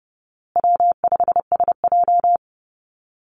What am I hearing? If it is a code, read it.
Morse code, heard as a keyed tone, about 30 wpm, 700 Hz. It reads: W5SJ
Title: Easy Call Signs Repeated 3x - 30wpm